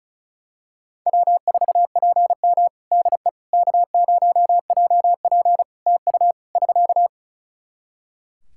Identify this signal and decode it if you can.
Morse code, heard as a keyed tone, about 34 wpm, 700 Hz. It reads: W4PM DE K0JP TU <SK>
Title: Practice QSO - Exchange 4 - Ending - 35wpm